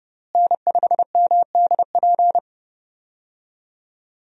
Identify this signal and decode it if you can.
Morse code, heard as a keyed tone, about 29 wpm, 700 Hz. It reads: N5MDP
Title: USA Call Signs - 5 Characters - Repeated 3x - 30wpm